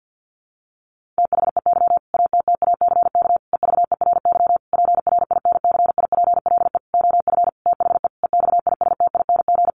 Morse code, heard as a keyed tone, about 50 wpm, 700 Hz. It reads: THEY ATTACK EVERY PRINCIPLE OF THE EXISTING
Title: Sentences from Top 1500 Words 50wpm